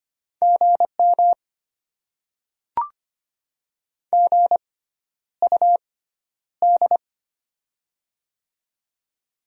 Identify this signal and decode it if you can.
Morse code, heard as a keyed tone, about 25 wpm, spaced as for 11 wpm, 700 Hz. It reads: GM GUD
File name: Top 20 QSO Elements - 6x Character Spacing - 25wpm